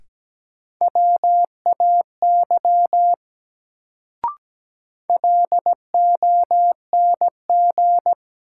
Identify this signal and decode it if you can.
Morse code, heard as a keyed tone, about 17 wpm, 700 Hz. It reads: WAY LONG
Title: Sets of 2 Words - New Word Drill - 17wpm